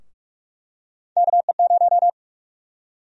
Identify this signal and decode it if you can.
Morse code, heard as a keyed tone, about 46 wpm, 700 Hz. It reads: KE0
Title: USA Call Sign Prefixes - Repeated 2x - 45wpm